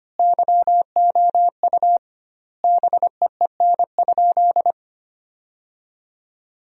Morse code, heard as a keyed tone, about 25 wpm, 700 Hz. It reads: YOU BEEN?
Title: 2x Word-Spacing - Sentences from Top 400 Words - 25wpm